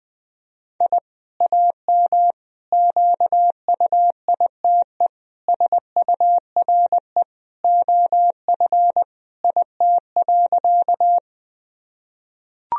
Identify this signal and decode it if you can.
Morse code, heard as a keyed tone, about 20 wpm, 700 Hz. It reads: I AM QUITE SURE OF IT.
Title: Sentences from Top 1000 Words - Review Part 2 - 20wpm